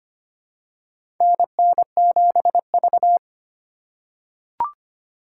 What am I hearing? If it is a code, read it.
Morse code, heard as a keyed tone, about 25 wpm, 700 Hz. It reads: NN7V
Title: USA Call Signs - 4 Characters - 8x Character Spacing - 25wpm